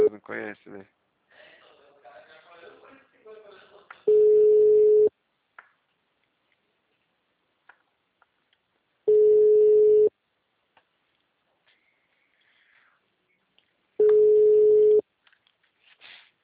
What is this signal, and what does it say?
Ringback tone (Europe, 425 Hz), with speech.
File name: out- -2038- - - .13993